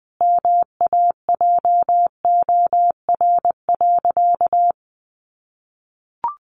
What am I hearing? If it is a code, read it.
Morse code, heard as a keyed tone, about 20 wpm, 700 Hz. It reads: MAJOR.
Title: Sentences from Top 600 Words 20wpm